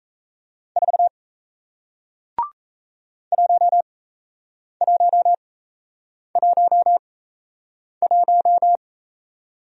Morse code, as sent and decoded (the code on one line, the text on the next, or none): ....- / .---- .---- .---- .----
4 1111